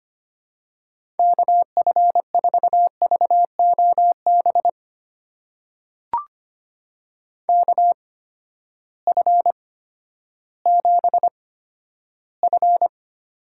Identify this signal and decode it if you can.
Morse code, heard as a keyed tone, about 25 wpm, spaced as for 12 wpm, 700 Hz. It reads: KF4VOB KF7F
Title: USA Call Signs - 6 Characters - 8x Character Spacing - 25wpm